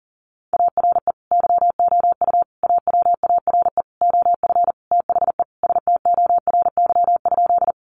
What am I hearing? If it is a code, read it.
Morse code, heard as a keyed tone, about 37 wpm, 700 Hz. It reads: ARE YOU AWARE OF THE STORY?
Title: Sentences from Top 1200 Words 40wpm